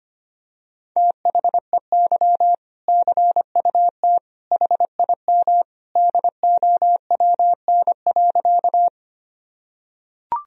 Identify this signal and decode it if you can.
Morse code, heard as a keyed tone, about 25 wpm, 700 Hz. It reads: THEY CUT HIM DOWN.
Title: Sentences from Top 1000 Words - Review Part 1 - 25wpm